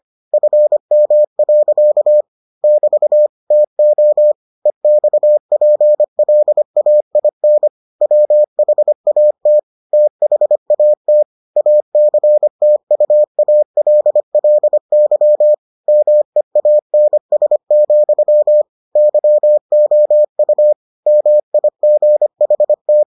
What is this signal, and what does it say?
Morse code, heard as a keyed tone, about 25 wpm, 600 Hz. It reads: FM. <BT> TO EXPLAIN WHAT THAT ACTUALLY MEANS, YOU MIGHT